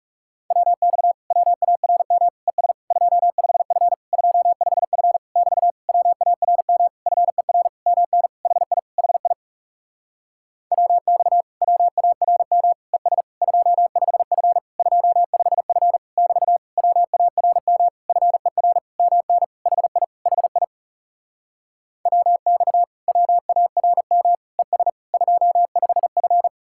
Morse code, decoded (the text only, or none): WX WARM ES 25F 25F <BT> WARM FER MN HI HI WX WARM ES 25F 25F <BT> WARM FER MN HI HI WX WARM ES 25F